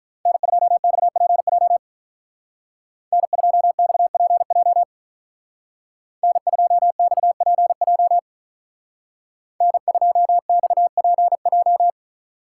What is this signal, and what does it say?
Morse code, heard as a keyed tone, about 43 wpm, 700 Hz. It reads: N2XPJ N2XPJ N2XPJ N2XPJ
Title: Speed Racing - USA 1x3 Call Signs 35wpm